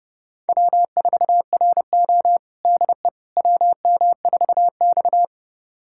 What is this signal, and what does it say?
Morse code, heard as a keyed tone, about 30 wpm, 700 Hz. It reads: W4RO DE WM4X
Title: USA Callsign DE Callsign - 4 Characters - 30wpm